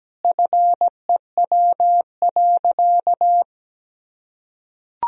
Morse code, heard as a keyed tone, about 17 wpm, 700 Hz. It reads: FEW.
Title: Sentences from Top 1000 Words - Review Part 1 - 17wpm